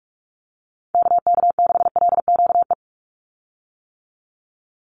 Morse code, heard as a keyed tone, about 44 wpm, 700 Hz. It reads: KK6LQE
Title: USA Call Signs 45wpm